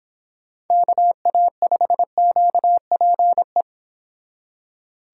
Morse code, heard as a keyed tone, about 26 wpm, 700 Hz. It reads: KA5QPE